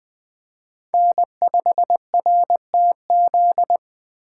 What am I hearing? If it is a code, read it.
Morse code, heard as a keyed tone, about 21 wpm, 700 Hz. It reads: N5RTZ
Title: USA Call Signs - 5 Characters - 3x Character Spacing - 20wpm